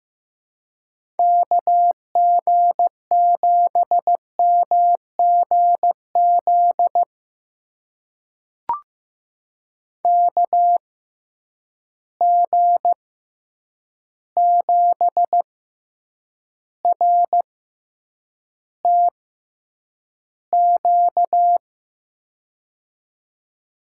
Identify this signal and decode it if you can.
Morse code, heard as a keyed tone, about 15 wpm, spaced as for 8 wpm, 700 Hz. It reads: KG7MGZ KG7RTQ
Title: USA Call Signs - 6x Character Spacing - 15wpm